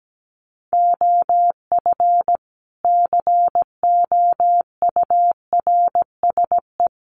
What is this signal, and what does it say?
Morse code, heard as a keyed tone, about 17 wpm, 700 Hz. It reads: OF COURSE
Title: A Sound Of Thunder 17wpm